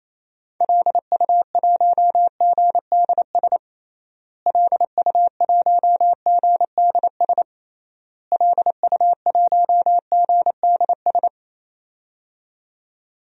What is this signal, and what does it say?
Morse code, heard as a keyed tone, about 28 wpm, 700 Hz. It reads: LU1GDS LU1GDS LU1GDS
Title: DXCC-100 Call Signs - 6 Characters - Repeated 3x - 28wpm